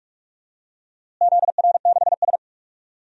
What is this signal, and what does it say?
Morse code, heard as a keyed tone, about 46 wpm, 700 Hz. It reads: ZR6S
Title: DXCC-100 Call Signs - 4 Characters - Repeated 3x - 45wpm